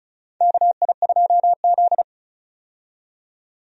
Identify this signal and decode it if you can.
Morse code, heard as a keyed tone, about 36 wpm, 700 Hz. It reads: KI2Z
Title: USA Call Signs - 4 Characters - 35wpm